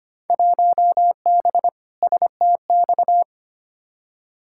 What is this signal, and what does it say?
Morse code, heard as a keyed tone, about 25 wpm, 700 Hz. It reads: 1B STX
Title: Field Day Exchanges - 25wpm